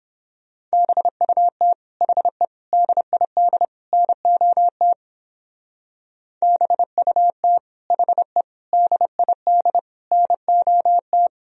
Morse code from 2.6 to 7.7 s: DID NOT BUT

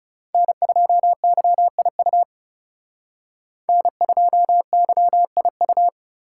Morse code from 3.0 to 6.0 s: N2YIU

